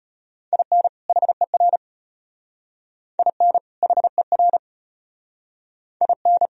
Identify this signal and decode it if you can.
Morse code, heard as a keyed tone, about 31 wpm, 700 Hz. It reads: IN HER IN HER IN